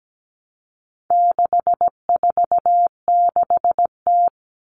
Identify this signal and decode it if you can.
Morse code, heard as a keyed tone, about 17 wpm, 700 Hz. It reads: 646T
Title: All Cut Numbers - 4-Digit 17wpm